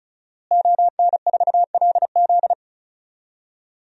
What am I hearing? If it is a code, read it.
Morse code, heard as a keyed tone, about 34 wpm, 700 Hz. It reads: ON4LZ
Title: DXCC-100 Call Signs - 5 Characters - Repeated 3x - 35wpm